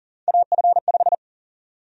Morse code, heard as a keyed tone, about 40 wpm, 700 Hz. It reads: AF5
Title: USA Call Sign Prefixes - Repeated 2x - 40wpm